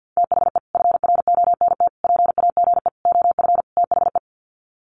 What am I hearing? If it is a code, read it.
Morse code, heard as a keyed tone, about 50 wpm, 700 Hz. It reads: THE FRONT PAGE OF THE
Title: Sets of 5 Words Encore 50wpm